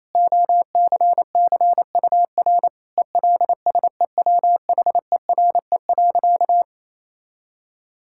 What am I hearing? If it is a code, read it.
Morse code, heard as a keyed tone, about 28 wpm, 700 Hz. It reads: OCCUR ELSEWHERE.
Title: Sentences from Top 1600 Words 28wpm